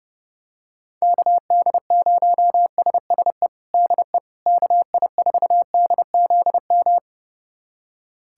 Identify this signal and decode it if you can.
Morse code, heard as a keyed tone, about 32 wpm, 700 Hz. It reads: KD0SSE DE KI4DZM
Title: USA Callsign DE Callsign 30wpm